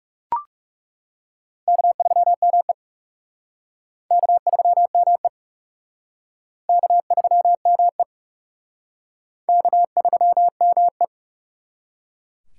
Morse code, heard as a keyed tone, about 37 wpm, 700 Hz. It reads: K3ME K3ME K3ME K3ME
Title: Speed Racing - USA Call Signs 30wpm